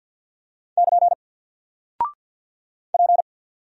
Morse code, heard as a keyed tone, about 48 wpm, 700 Hz. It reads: <KN> P
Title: Single Character (Letter:Number:Punctuation:Prosign) 50wpm